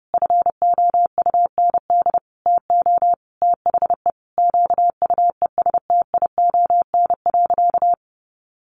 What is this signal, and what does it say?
Morse code, heard as a keyed tone, about 28 wpm, 700 Hz. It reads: FOUND TO THE QUESTION.
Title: Sentences from Top 1400 Words 30wpm